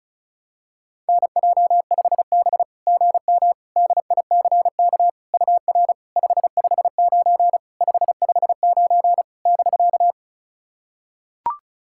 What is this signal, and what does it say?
Morse code, heard as a keyed tone, about 36 wpm, 690 Hz. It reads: NJ5B GM DICK UR 559 559 <BK>
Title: SOTA QSO Practice 35wpm